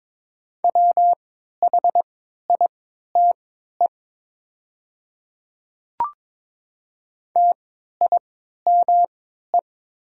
Morse code, heard as a keyed tone, about 22 wpm, spaced as for 13 wpm, 700 Hz. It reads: WHITE TIME
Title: 3x Character Spacing - Top 300 Words - 22wpm